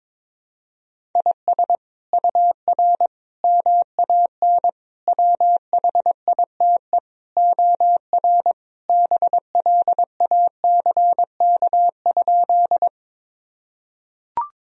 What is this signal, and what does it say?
Morse code, heard as a keyed tone, about 22 wpm, 700 Hz. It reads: IS UR MAN WHITE OR BLACK?
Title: Sentences from Top 400 Words and Abbreviations 22wpm